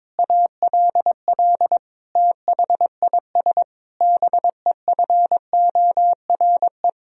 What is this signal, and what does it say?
Morse code, heard as a keyed tone, about 22 wpm, 700 Hz. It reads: ALL THIS BEFORE
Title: Sentences from Top 1000 Words - Review Part 4 - 22wpm